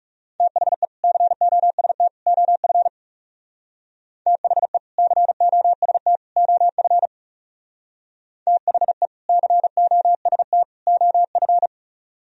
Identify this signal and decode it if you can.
Morse code, heard as a keyed tone, about 36 wpm, 700 Hz. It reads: THE COST OF THE COST OF THE COST OF
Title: Speed-Racing - Sets of 3 Words 30wpm